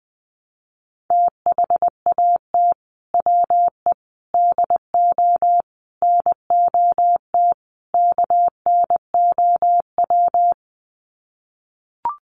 Morse code, heard as a keyed tone, about 20 wpm, 700 Hz. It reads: THAT WE DO NOT KNOW
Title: Sets of 5 Words Encore2 20wpm